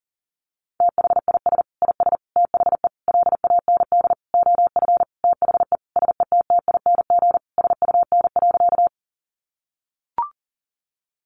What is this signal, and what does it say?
Morse code, heard as a keyed tone, about 40 wpm, 700 Hz. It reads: THIS IS THE LAND OF THE SETTING SUN.
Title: Sentences from Top 1300 Words 40wpm